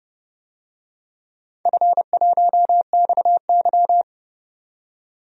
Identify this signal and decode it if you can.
Morse code, heard as a keyed tone, about 30 wpm, 700 Hz. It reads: F1XY
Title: DXCC-100 Call Signs - 4 Characters - 30wpm